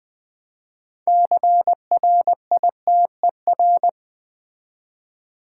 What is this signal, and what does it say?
Morse code, heard as a keyed tone, about 20 wpm, 700 Hz. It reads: CRITER
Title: Build A Word - 3 Letters At A Time - 9-Letter Words - 20wpm